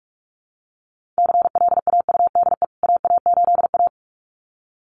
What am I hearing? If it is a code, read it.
Morse code, heard as a keyed tone, about 44 wpm, 700 Hz. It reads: CLAUDE AA8A
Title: CWT Exchange 45wpm